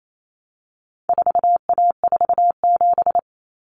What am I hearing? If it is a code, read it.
Morse code, heard as a keyed tone, about 28 wpm, 700 Hz. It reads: 4A47
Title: Common Cut Numbers - 4-Digit 28wpm